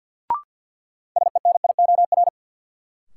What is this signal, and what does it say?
Morse code, heard as a keyed tone, about 53 wpm, 700 Hz. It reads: SENIOR
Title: Sets of 5 Words Encore - New Word Drill - 50wpm